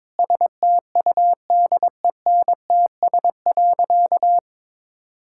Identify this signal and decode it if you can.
Morse code, heard as a keyed tone, about 22 wpm, 700 Hz. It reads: STUDENTS.
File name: practice / 22wpm / Sentences with 2-Word Phrases - 22wpm